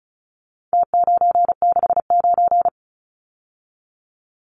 Morse code, heard as a keyed tone, about 34 wpm, 700 Hz. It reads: T969
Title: Numbers - Contest Style - 4-Digit 35wpm